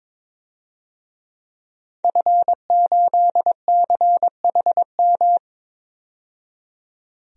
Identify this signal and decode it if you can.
Morse code, heard as a keyed tone, about 22 wpm, 700 Hz. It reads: F8CHM
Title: DXCC-100 Call Signs - 5 Characters - 22wpm